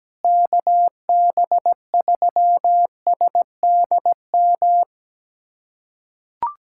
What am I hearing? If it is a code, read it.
Morse code, heard as a keyed tone, about 17 wpm, 700 Hz. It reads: KB3SDM
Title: USA Callsign DE Callsign - 6 Characters - 17wpm